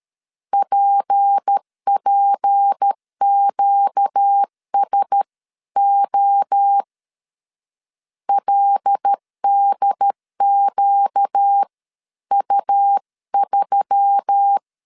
Morse code, heard as a keyed tone, about 13 wpm, spaced as for 11 wpm, 800 Hz. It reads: PPQSO LDQU3
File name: LEZIONI / 6 PKX0 / 6 P 12wpm 2033